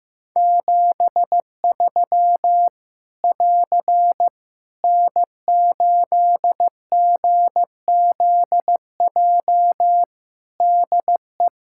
Morse code, heard as a keyed tone, about 15 wpm, 700 Hz. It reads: 73 <AR> N8GZJ DE
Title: Practice QSO - Exchange 4 - Ending - 15wpm